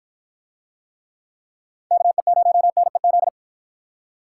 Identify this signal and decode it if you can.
Morse code, heard as a keyed tone, about 53 wpm, 700 Hz. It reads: KE0NEZ